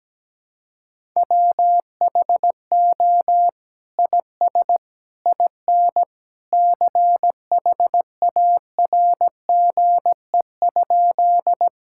Morse code, heard as a keyed tone, about 17 wpm, 700 Hz. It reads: WHO IS IN CHARGE?
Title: Sentences from Top 1000 Words - Review Part 6 - 17wpm